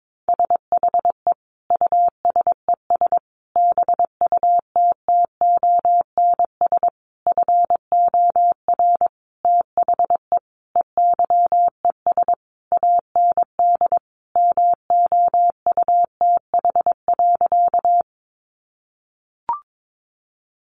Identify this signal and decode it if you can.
Morse code, heard as a keyed tone, about 22 wpm, 700 Hz. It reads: SHE USES BUTTONS FOR THE EYES AND MOUTH.
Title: Easy Phrases 22wpm